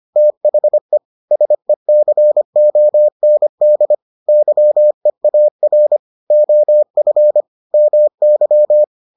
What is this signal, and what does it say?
Morse code, heard as a keyed tone, about 25 wpm, 600 Hz. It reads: THE SECOND YEAR OF MY